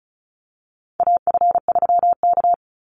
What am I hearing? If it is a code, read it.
Morse code, heard as a keyed tone, about 34 wpm, 700 Hz. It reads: AF3K
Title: Call Signs Repeated 3x - 35wpm